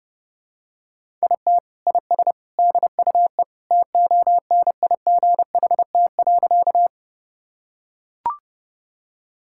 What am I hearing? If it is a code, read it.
Morse code, heard as a keyed tone, about 30 wpm, 700 Hz. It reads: IT IS DUE TONIGHT.
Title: Sentences from Top 1000 Words - Review Part 6 - 30wpm